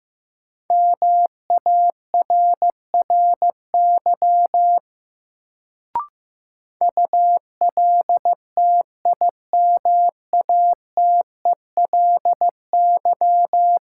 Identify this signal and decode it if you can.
Morse code, heard as a keyed tone, about 15 wpm, 700 Hz. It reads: MARRY ULTIMATELY
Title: Sentences from Top 1200 Words - New Word Drill Alt - 15wpm